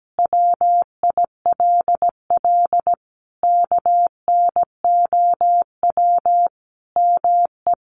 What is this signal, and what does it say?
Morse code, heard as a keyed tone, about 17 wpm, 700 Hz. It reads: WILL KNOW ME
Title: Sentences from Top 400 Words 17wpm